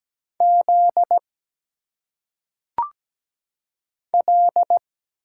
Morse code, heard as a keyed tone, about 17 wpm, 700 Hz. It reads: Z L
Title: Single Letters - Phonetic Pronunciation 17wpm